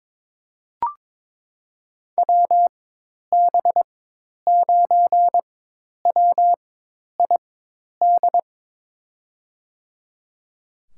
Morse code, heard as a keyed tone, about 22 wpm, spaced as for 10 wpm, 700 Hz. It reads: WB9WID
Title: USA Call Signs - 6 Characters - 4x Character Spacing - 22wpm